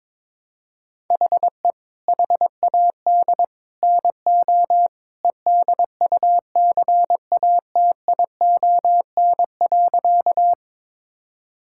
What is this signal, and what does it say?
Morse code, heard as a keyed tone, about 21 wpm, 700 Hz. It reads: HE HAD NO EDUCATION.
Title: Sentences from Top 1000 Words - Review Part 6 - 22wpm